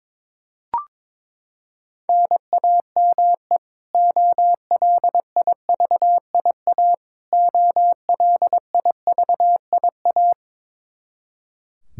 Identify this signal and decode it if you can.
Morse code, heard as a keyed tone, about 22 wpm, 700 Hz. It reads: NAME OLIVIA OLIVIA
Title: Practice QSO - Exchange 1 - Beginning - 22wpm